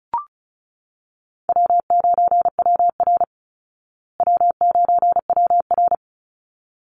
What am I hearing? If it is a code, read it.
Morse code, heard as a keyed tone, about 34 wpm, 700 Hz. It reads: W9WR W9WR
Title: Call Signs Repeated 3x - 35wpm